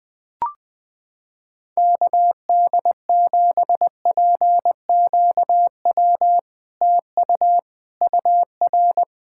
Morse code, heard as a keyed tone, about 20 wpm, 700 Hz. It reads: KD7PQW TU UR
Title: SOTA QSO Practice 20wpm